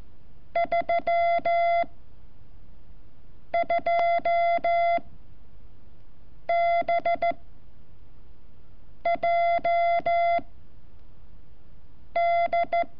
Morse code, heard as a keyed tone, about 14 wpm, 680 Hz. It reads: 3 2 B J D
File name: curso / ejercicio 28